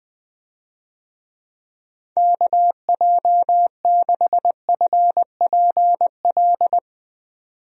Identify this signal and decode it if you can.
Morse code, heard as a keyed tone, about 21 wpm, 700 Hz. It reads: KJ6FPL
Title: USA Call Signs - 6 Characters - 20wpm